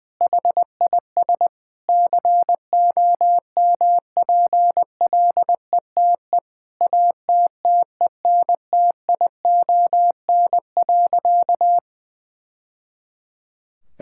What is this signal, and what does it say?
Morse code, heard as a keyed tone, about 20 wpm, 700 Hz. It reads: HIS COMPLETE ATTENTION.
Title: Sentences from Top 1000 Words - Review Part 6 - 20wpm